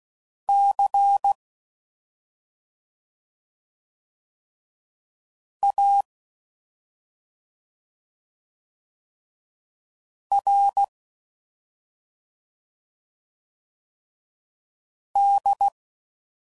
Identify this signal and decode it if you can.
Morse code, heard as a keyed tone, about 16 wpm, 800 Hz. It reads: C A R D
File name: Suffixes / Special ING's / Sngl ING's